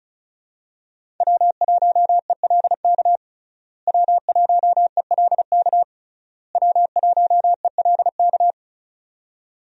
Morse code, heard as a keyed tone, about 36 wpm, 700 Hz. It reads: W1ELK W1ELK W1ELK
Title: USA Call Signs - 5 Characters - Repeated 3x - 35wpm